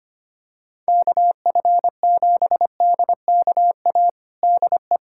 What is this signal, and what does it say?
Morse code, heard as a keyed tone, about 25 wpm, 700 Hz. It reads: KF7DKA DE
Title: USA Callsign DE Callsign - 6 Characters - 25wpm